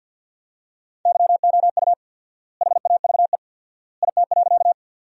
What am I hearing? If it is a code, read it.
Morse code, heard as a keyed tone, about 50 wpm, 700 Hz. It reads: YOU HAVE IT.
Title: 4x Word-Spacing - Sentences from Top 100 Words - 50wpm